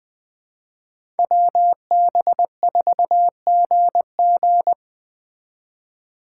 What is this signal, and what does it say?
Morse code, heard as a keyed tone, about 21 wpm, 700 Hz. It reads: WB4GG